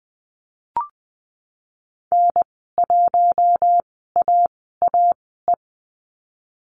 Morse code, heard as a keyed tone, about 20 wpm, spaced as for 14 wpm, 700 Hz. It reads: N1AAE
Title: 082 Lesson 14g - Number 2 - Call Signs 20wpm